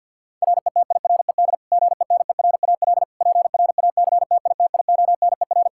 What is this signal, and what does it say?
Morse code, heard as a keyed tone, about 52 wpm, 700 Hz. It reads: RETIRED GENERAL PRACTITIONER
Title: Practice QSO - Exchange 3 - More Info - 50wpm